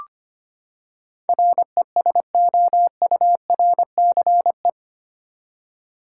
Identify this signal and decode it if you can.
Morse code, heard as a keyed tone, about 25 wpm, 700 Hz. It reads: RESOURCE
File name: Eight Letter Words - 25wpm